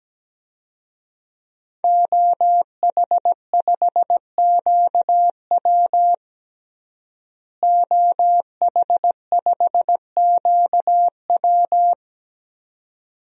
Morse code, heard as a keyed tone, about 17 wpm, 700 Hz. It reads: OH5QW OH5QW
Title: DXCC-100 Call Signs - 5 Characters - Repeated 2x - 17wpm